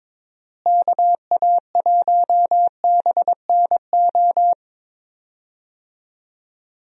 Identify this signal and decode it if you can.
Morse code, heard as a keyed tone, about 22 wpm, 700 Hz. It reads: KA1BNO
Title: USA Call Signs - 6 Characters - Repeated 3x - 22wpm